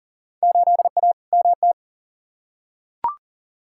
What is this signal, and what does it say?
Morse code, heard as a keyed tone, about 39 wpm, 700 Hz. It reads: 8A MT
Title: Field Day Exchanges Encore - 40wpm